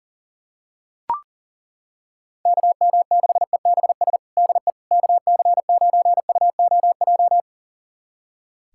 Morse code, heard as a keyed tone, about 37 wpm, 700 Hz. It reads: KM6EBS DE KC9UOJ
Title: USA Callsign DE Callsign - 6 Characters - 40wpm